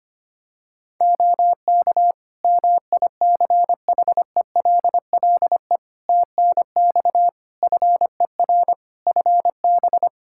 Morse code, heard as a keyed tone, about 25 wpm, 700 Hz. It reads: OK MICHELLE TNX FER FB
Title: Practice QSO - Exchange 4 - Ending - 25wpm